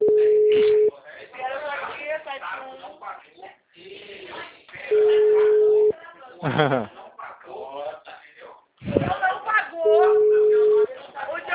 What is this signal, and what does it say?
Ringback tone (Europe, 425 Hz), with speech.
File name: out- -2024